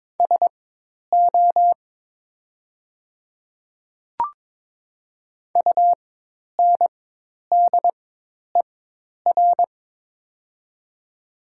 Morse code, heard as a keyed tone, about 22 wpm, spaced as for 10 wpm, 700 Hz. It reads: SO UNDER
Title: 4x Character Spacing - Top 300 Words - 22wpm